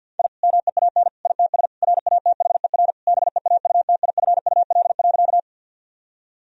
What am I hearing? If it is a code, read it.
Morse code, heard as a keyed tone, about 50 wpm, 700 Hz. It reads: I MEAN ITS RATHER BEAUTIFUL.
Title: Sentences from Top 1000 Words - Review Part 4 - 50wpm